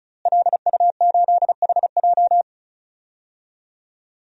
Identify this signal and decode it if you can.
Morse code, heard as a keyed tone, about 36 wpm, 700 Hz. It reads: LU8HJ
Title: DXCC-100 Call Signs - 5 Characters - Repeated 3x - 35wpm